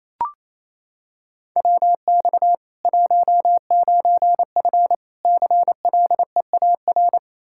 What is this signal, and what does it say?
Morse code, heard as a keyed tone, about 28 wpm, 700 Hz. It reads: WX 19F CLEAR
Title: Practice QSO - Exchange 2 - Info - 28wpm